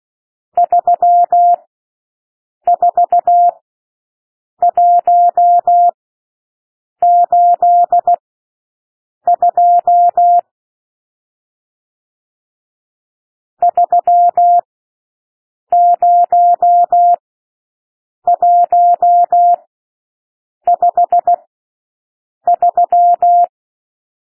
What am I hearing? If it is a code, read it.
Morse code, heard as a keyed tone, about 16 wpm, spaced as for 6 wpm, 700 Hz. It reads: 34182 30153
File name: SOUND NUMBERS 16 wpm